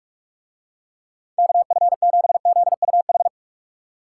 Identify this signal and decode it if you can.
Morse code, heard as a keyed tone, about 44 wpm, 700 Hz. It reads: KF7ZUH